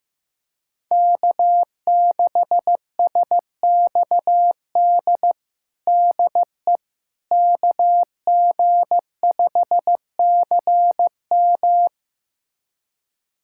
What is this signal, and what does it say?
Morse code, heard as a keyed tone, about 15 wpm, 700 Hz. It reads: K6SXD DE KG5CM
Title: USA Callsign DE Callsign - 5 Characters - 15wpm